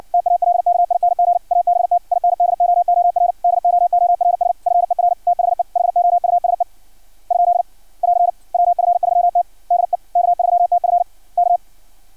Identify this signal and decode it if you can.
Morse code, heard as a keyed tone, about 56 wpm, 700 Hz. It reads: TALKING ABT INDOOR DOORS FER THE HOUSE ? <AR> WF3T DE K3TF K
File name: archived / CRSnet